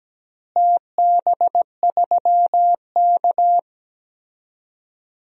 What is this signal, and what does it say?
Morse code, heard as a keyed tone, about 17 wpm, 700 Hz. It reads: TB3K
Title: DXCC-100 Call Signs - 4 Characters - Repeated 2x - 17wpm